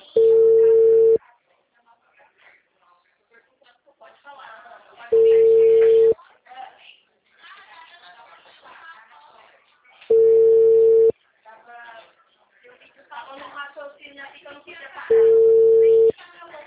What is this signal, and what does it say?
Ringback tone (China, 450 Hz), with speech.
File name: out- -2012- - - .49303